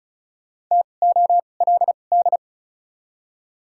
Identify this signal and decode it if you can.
Morse code, heard as a keyed tone, about 36 wpm, spaced as for 25 wpm, 700 Hz. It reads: TOLD
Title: 2x Character Spacing - Top 500 Words - 35wpm